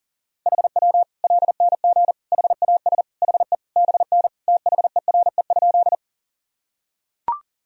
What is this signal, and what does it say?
Morse code, heard as a keyed tone, about 40 wpm, 700 Hz. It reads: HW LNG HAS HE BN THERE?
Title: Sentences from Top 100 Words and Abbreviations 40wpm